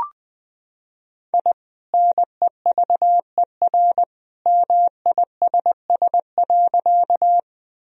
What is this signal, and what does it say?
Morse code, heard as a keyed tone, about 20 wpm, 700 Hz. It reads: I NEVER MISS.
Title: Sentences from Top 1000 Words - Review Part 3 - 20wpm